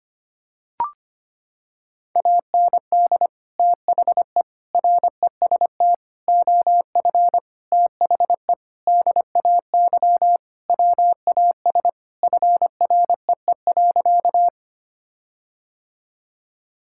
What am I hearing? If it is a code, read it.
Morse code, heard as a keyed tone, about 25 wpm, 700 Hz. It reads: AND THE REST OF THE DAY WAS FREE.
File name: practice / 25wpm / Sentences with 4-Word Phrases - 25wpm